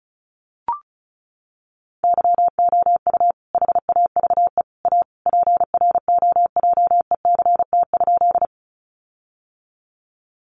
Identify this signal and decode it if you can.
Morse code, heard as a keyed tone, about 34 wpm, 700 Hz. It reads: YOU HAVE A PROJECT?
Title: Sentences from Top 1400 Words 35wpm